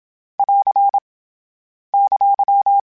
Morse code, heard as a keyed tone, about 27 wpm, 800 Hz. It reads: <AR> !
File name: Short Runs / Short Runs-mp3s / Balanced @25